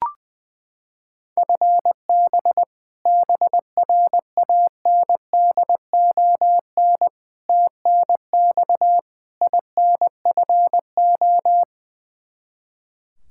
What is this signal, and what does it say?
Morse code, heard as a keyed tone, about 20 wpm, 700 Hz. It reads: FB BRANDON TNX INFO
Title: Practice QSO - Exchange 3 - More Info - 20wpm